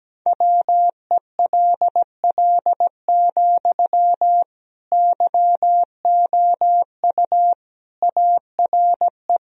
Morse code, heard as a keyed tone, about 17 wpm, 700 Hz. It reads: WELL, YOU ARE
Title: Sentences from Top 1000 Words - Review Part 2 - 17wpm